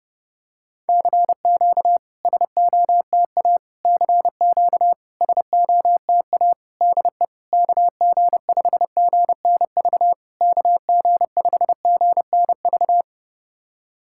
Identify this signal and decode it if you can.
Morse code, heard as a keyed tone, about 30 wpm, 700 Hz. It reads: CQ SOTA CQ SOTA DE KG5GNV KG5GNV
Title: SOTA QSO Practice 30wpm